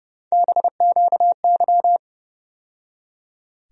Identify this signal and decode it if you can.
Morse code, heard as a keyed tone, about 30 wpm, 700 Hz. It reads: BQY
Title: USA Call Sign Postfixes - 3 Letters 30wpm